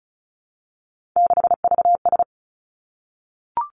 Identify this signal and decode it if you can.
Morse code, heard as a keyed tone, about 36 wpm, 700 Hz. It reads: 6VS
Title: Three Challenging Letter-Number Combinations 35wpm